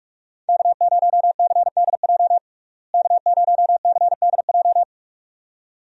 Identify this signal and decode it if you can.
Morse code, heard as a keyed tone, about 42 wpm, 700 Hz. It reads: K0CDJ K0CDJ
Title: USA Call Signs - 5 Characters - Repeated 2x - 45wpm